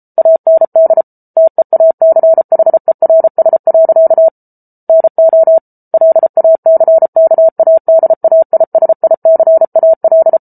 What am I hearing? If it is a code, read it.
Morse code, heard as a keyed tone, about 34 wpm, 650 Hz. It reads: AND TEACHERS. NO LACKADAISICAL